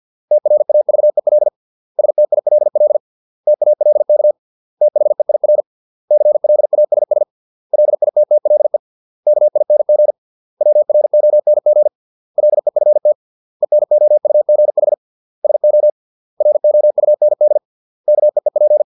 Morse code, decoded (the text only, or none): TRAVEL STILL TALK THEIR CLASS LITTLE KING WRONG LEFT ENOUGH SO ROUND KEEP